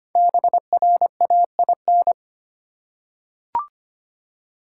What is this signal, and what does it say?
Morse code, heard as a keyed tone, about 25 wpm, 700 Hz. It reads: BRAIN
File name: 5 Letter Words - First 3 Letters Given - 25wpm